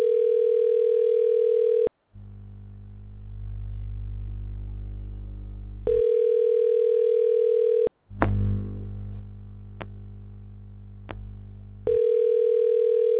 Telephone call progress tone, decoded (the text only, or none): ringback tone